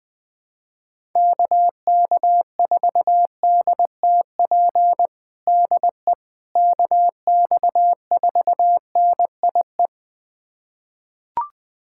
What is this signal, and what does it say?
Morse code, heard as a keyed tone, about 20 wpm, 700 Hz. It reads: KK4DTP DE KX4NIE
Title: USA Callsign DE Callsign - 6 Characters - 20wpm